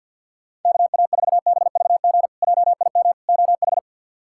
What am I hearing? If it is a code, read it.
Morse code, heard as a keyed tone, about 48 wpm, 700 Hz. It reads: KA4BVG JIM OH
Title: K1USN SST QSO Practice - 50wpm